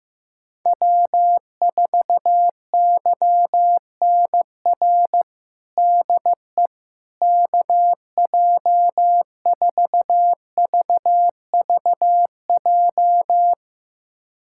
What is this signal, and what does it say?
Morse code, heard as a keyed tone, about 15 wpm, 700 Hz. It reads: W4YNR DE KJ4VVJ
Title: USA Callsign DE Callsign 15wpm